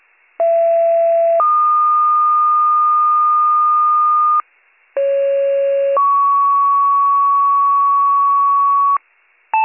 Two-tone page, heard as a one-second tone, then a three-second tone, about 670 then 1220 Hz, about 550 then 1120 Hz. It is heard from 0.4 to 9.0 s.